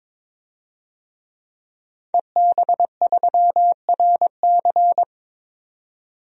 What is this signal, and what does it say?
Morse code, heard as a keyed tone, about 22 wpm, 700 Hz. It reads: EB3RC
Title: DXCC-100 Call Signs - 5 Characters - 22wpm